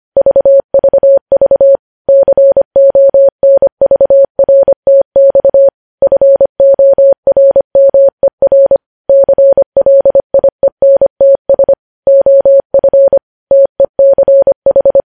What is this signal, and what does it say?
Morse code, heard as a keyed tone, about 25 wpm, 550 Hz. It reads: VVV CONVRTX FORMER CLIENTS OF TECH
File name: podcast / News headlines-1x-25x25